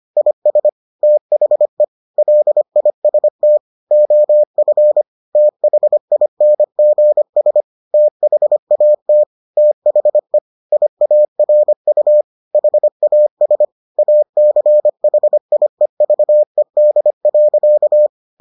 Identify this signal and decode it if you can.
Morse code, heard as a keyed tone, about 25 wpm, 600 Hz. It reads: IS THE LIST OF THINGS THAT THE IARU HAS ACHIEVED.